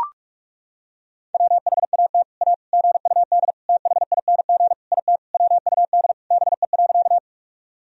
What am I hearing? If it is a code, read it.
Morse code, heard as a keyed tone, about 41 wpm, 700 Hz. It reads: WHAT A GUD THING IT WUD BE.
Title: Sentences from Top 200 Words and Abbreviations 45wpm